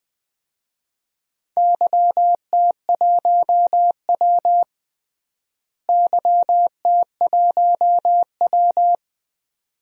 Morse code, heard as a keyed tone, about 20 wpm, 700 Hz. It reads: YT1W YT1W
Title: DXCC-100 Call Signs - 4 Characters - Repeated 2x - 20wpm